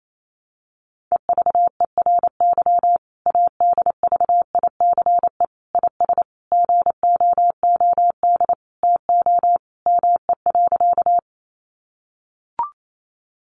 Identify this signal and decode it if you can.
Morse code, heard as a keyed tone, about 28 wpm, 700 Hz. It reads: EVERY ADVICE IS GOOD TO ME.
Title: Sentences from Top 1100 Words 28wpm